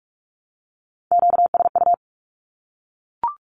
Morse code, heard as a keyed tone, about 44 wpm, 700 Hz. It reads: QSU
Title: Q-Codes Repeated 3x 45wpm